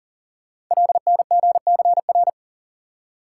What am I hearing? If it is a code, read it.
Morse code, heard as a keyed tone, about 39 wpm, 690 Hz. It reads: LNGCR
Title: Code Groups - 5 Letters - 40wpm